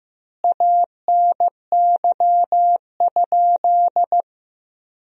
Morse code, heard as a keyed tone, about 15 wpm, 700 Hz. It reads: ANY?
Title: 2x Word-Spacing - Sentences from Top 300 Words - 15wpm